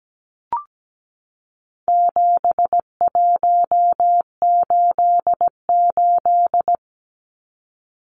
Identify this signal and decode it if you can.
Morse code, heard as a keyed tone, about 17 wpm, 700 Hz. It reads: 7188
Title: Numbers - 4-Digit 17wpm